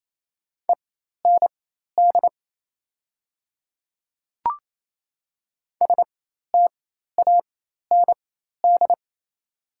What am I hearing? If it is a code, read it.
Morse code, heard as a keyed tone, about 28 wpm, spaced as for 13 wpm, 700 Hz. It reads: END STAND
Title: 4x Character Spacing - Top 300 Words - 28wpm